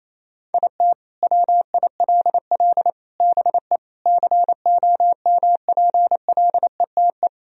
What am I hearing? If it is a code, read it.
Morse code, heard as a keyed tone, about 28 wpm, 700 Hz. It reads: IT WILL BE COMPLETE
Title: Sentences from Top 1000 Words - Review Part 2 - 28wpm